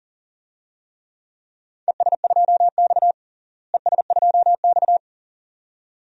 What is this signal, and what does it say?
Morse code, heard as a keyed tone, about 40 wpm, 700 Hz. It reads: ES2X ES2X
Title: DXCC-100 Call Signs - 4 Characters - Repeated 2x - 40wpm